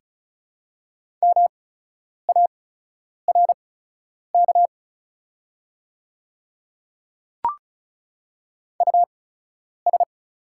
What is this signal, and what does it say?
Morse code, heard as a keyed tone, about 35 wpm, spaced as for 10 wpm, 700 Hz. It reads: MARK US
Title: 8x Character Spacing - Top 300 Words - 35wpm